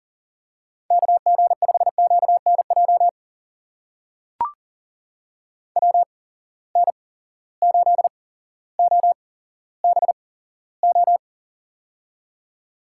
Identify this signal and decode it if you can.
Morse code, heard as a keyed tone, about 37 wpm, spaced as for 17 wpm, 700 Hz. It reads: KG5QNJ WN8OBO
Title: USA Call Signs - 8x Character Spacing - 40wpm